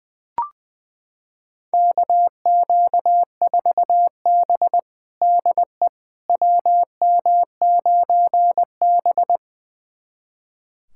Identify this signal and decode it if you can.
Morse code, heard as a keyed tone, about 21 wpm, 700 Hz. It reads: KQ4B DE WM9B
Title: USA Callsign DE Callsign - 4 Characters - 20wpm